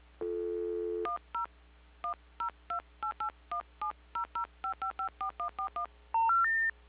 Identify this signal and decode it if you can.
DTMF, heard as telephone touch tones, tones of about 110 ms. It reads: 10 1028817005554141
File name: recordings / PortlandUnion Millennium